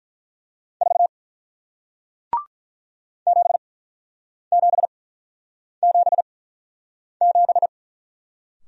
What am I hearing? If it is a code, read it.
Morse code, heard as a keyed tone, about 48 wpm, spaced as for 9 wpm, 700 Hz. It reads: 4 7777